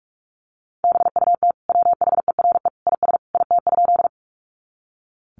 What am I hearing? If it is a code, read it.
Morse code, heard as a keyed tone, about 44 wpm, 700 Hz. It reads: BUT WHERE IS IT?
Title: Sentences from Top 200 Words 45wpm